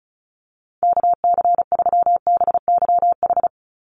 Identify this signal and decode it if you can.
Morse code, heard as a keyed tone, about 36 wpm, 700 Hz. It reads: KC3BYH
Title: USA Call Signs 35wpm